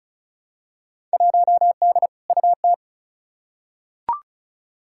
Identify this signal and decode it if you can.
Morse code, heard as a keyed tone, about 36 wpm, 700 Hz. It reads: 1D UT
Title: Field Day Exchanges - 35wpm